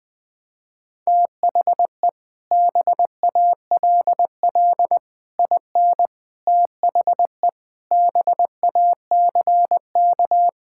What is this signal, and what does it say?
Morse code, heard as a keyed tone, about 21 wpm, 700 Hz. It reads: THE BALL IN THE BACK
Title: Sets of 8 Words - 20wpm